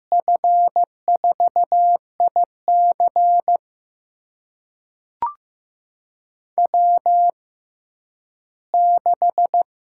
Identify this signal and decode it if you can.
Morse code, heard as a keyed tone, about 15 wpm, spaced as for 10 wpm, 700 Hz. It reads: F4IC W6
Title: USA Call Signs - 5 Characters - 6x Character Spacing - 15wpm